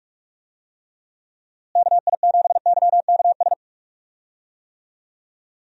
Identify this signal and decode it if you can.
Morse code, heard as a keyed tone, about 46 wpm, 700 Hz. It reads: KI7YKS
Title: USA Call Signs - 6 Characters - Repeated 3x - 45wpm